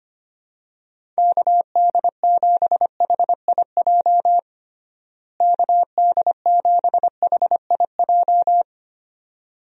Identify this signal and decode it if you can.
Morse code, heard as a keyed tone, about 25 wpm, 700 Hz. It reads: KD7HIJ KD7HIJ
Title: USA Call Signs - 6 Characters - Repeated 2x - 25wpm